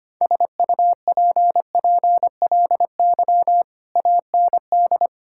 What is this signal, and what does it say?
Morse code, heard as a keyed tone, about 25 wpm, 700 Hz. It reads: SUPPLY AND
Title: Sentences from Top 1000 Words - Review Part 6 - 25wpm